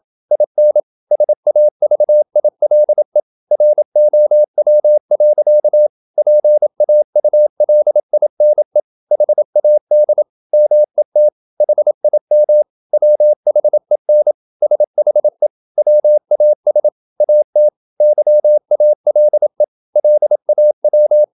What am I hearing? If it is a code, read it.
Morse code, heard as a keyed tone, about 27 wpm, 600 Hz. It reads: IN SAVILE ROW. PAULINE HAD MET HIM WHEN SHE WAS AT YALE LAW
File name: Never 0027wpm